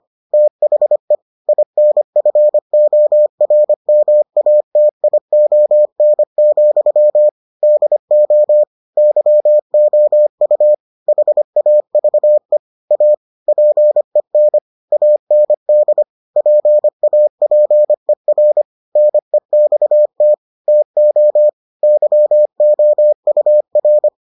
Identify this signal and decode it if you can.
Morse code, heard as a keyed tone, about 25 wpm, 600 Hz. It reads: THE INFORMATION, DO YOU HAVE A PEN AND PAPER NEXT TO YOUR